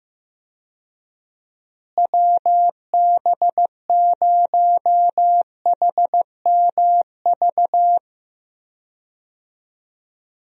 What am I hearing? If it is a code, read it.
Morse code, heard as a keyed tone, about 15 wpm, 700 Hz. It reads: WB0HMV
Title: USA Call Signs - 6 Characters - Repeated 3x - 15wpm